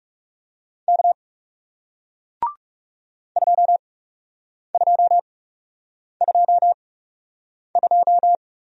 Morse code, heard as a keyed tone, about 40 wpm, spaced as for 8 wpm, 700 Hz. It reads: K 2222